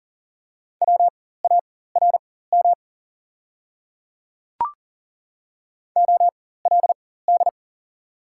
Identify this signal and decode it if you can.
Morse code, heard as a keyed tone, about 40 wpm, spaced as for 19 wpm, 700 Hz. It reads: WARM OLD
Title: 4x Character Spacing - Top 500 Words - 40wpm